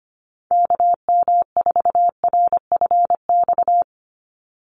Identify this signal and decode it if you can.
Morse code, heard as a keyed tone, about 25 wpm, 700 Hz. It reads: KM4RFX
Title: USA Call Signs 25wpm